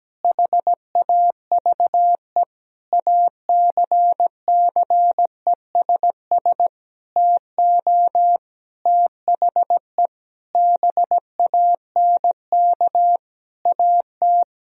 Morse code, heard as a keyed tone, about 17 wpm, 700 Hz. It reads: HAVE ACCESS TO THE BANK AT
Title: Sentences from Top 1000 Words - Review Part 4 - 17wpm